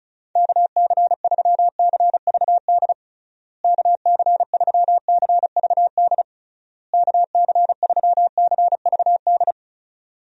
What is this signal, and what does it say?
Morse code, heard as a keyed tone, about 34 wpm, 700 Hz. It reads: KC3CVD KC3CVD KC3CVD
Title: USA Call Signs - 6 Characters - Repeated 3x - 35wpm